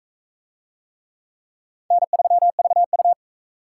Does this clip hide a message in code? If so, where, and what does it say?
Yes: Morse code from 1.9 to 3.1 s, heard as a keyed tone, about 36 wpm, 700 Hz. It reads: N3VU